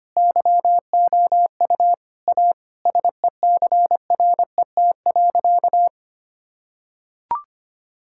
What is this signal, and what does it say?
Morse code, heard as a keyed tone, about 25 wpm, 690 Hz. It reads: YOU A SECRET.
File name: practice / 25wpm / Sentences with 2-Word Phrases - 25wpm